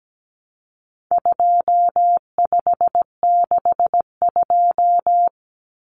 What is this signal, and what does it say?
Morse code, heard as a keyed tone, about 17 wpm, 700 Hz. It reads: 2562
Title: Common Cut Numbers - 4-Digit 17wpm